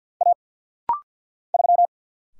Morse code, heard as a keyed tone, about 50 wpm, 700 Hz. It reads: A 3
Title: Single Letter-Number - Rapid-Fire 50wpm